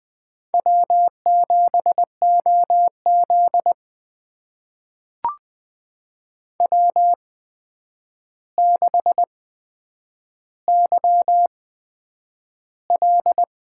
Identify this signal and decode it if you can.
Morse code, heard as a keyed tone, about 20 wpm, spaced as for 9 wpm, 700 Hz. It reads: W7OZ W6YL
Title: USA Call Signs - 4 Characters - 8x Character Spacing - 20wpm